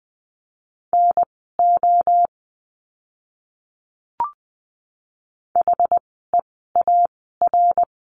Morse code, heard as a keyed tone, about 20 wpm, spaced as for 15 wpm, 700 Hz. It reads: NO HEAR
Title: 230 Lesson 37f - Letter x - Words 20wpm